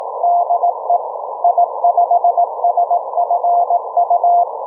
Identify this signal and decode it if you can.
Morse code, heard as a keyed tone, about 18 wpm, 690 Hz. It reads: DE I5SFU